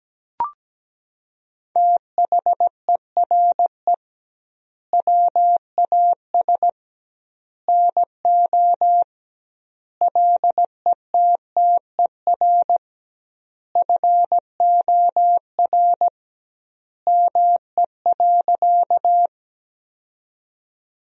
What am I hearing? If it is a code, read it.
Morse code, heard as a keyed tone, about 17 wpm, 700 Hz. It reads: THERE WAS NO LETTER FOR ME.
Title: 2x Word-Spacing - Sentences from Top 300 Words - 17wpm